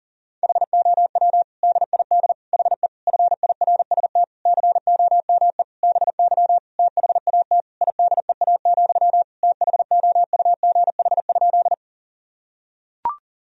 Morse code, heard as a keyed tone, about 40 wpm, 700 Hz. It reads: HOW DID HE FIRST COME BY THAT IDEA, THOUGH?
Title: Sentences from Top 1000 Words - Review Part 2 - 40wpm